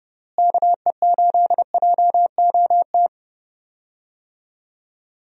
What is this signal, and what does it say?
Morse code, heard as a keyed tone, about 30 wpm, 700 Hz. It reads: KE8JOT
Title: USA Call Signs - 6 Characters - Repeated 3x - 30wpm